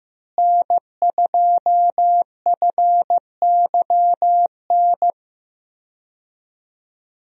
Morse code, heard as a keyed tone, about 15 wpm, 700 Hz. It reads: N2FYN
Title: USA Call Signs - 5 Characters - Repeated 2x - 15wpm